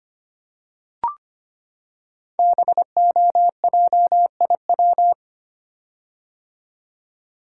Morse code, heard as a keyed tone, about 25 wpm, 700 Hz. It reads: BOJIW
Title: Code Groups - 5 Letters - 25wpm